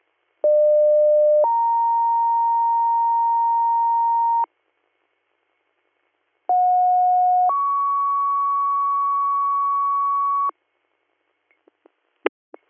Two-tone page, heard as a one-second tone, then a three-second tone, about 600 then 930 Hz, about 730 then 1120 Hz. It is heard from 0.4 to 10.5 s.